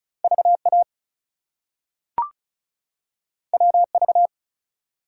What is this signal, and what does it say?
Morse code, heard as a keyed tone, about 36 wpm, 700 Hz. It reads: VA WV
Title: US States and Canadian Provinces 35wpm